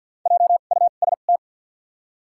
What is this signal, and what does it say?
Morse code, heard as a keyed tone, about 53 wpm, spaced as for 37 wpm, 700 Hz. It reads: JUST
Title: 2x Character Spacing - Top 500 Words - 50wpm